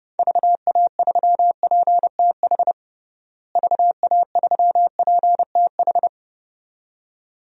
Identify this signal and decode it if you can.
Morse code, heard as a keyed tone, about 29 wpm, 700 Hz. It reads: VA3PTH VA3PTH
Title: DXCC-100 Call Signs - 6 Characters - Repeated 2x - 30wpm